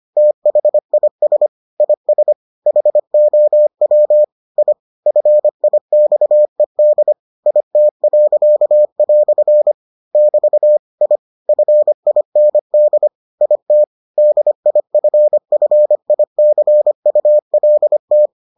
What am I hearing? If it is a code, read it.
Morse code, heard as a keyed tone, about 25 wpm, 600 Hz. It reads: THIS IS HOW I FIXED IT." <BT> I FIND IT DIFFICULT